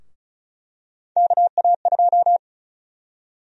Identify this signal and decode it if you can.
Morse code, heard as a keyed tone, about 34 wpm, 700 Hz. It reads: KA2
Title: USA Call Sign Prefixes - Repeated 2x - 35wpm